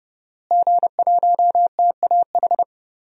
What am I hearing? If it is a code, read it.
Morse code, heard as a keyed tone, about 32 wpm, 700 Hz. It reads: G1TAH
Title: DXCC-100 Call Signs - 5 Characters - Repeated 3x - 30wpm